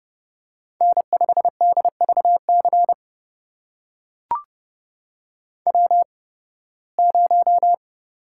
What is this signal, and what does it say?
Morse code, heard as a keyed tone, about 30 wpm, spaced as for 20 wpm, 700 Hz. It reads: N5DVC W0
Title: USA Call Signs - 5 Characters - 8x Character Spacing - 30wpm